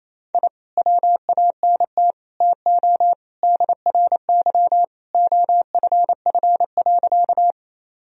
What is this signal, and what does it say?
Morse code, heard as a keyed tone, about 28 wpm, 700 Hz. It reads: I WANT TO DRY OFF.
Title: Sentences from Top 1000 Words - Review Part 3 - 28wpm